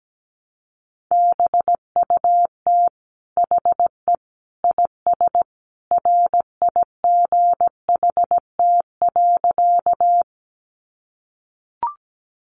Morse code, heard as a keyed tone, about 17 wpm, 700 Hz. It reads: BUT HE IS RIGHT.
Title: Sentences from Top 200 Words 17wpm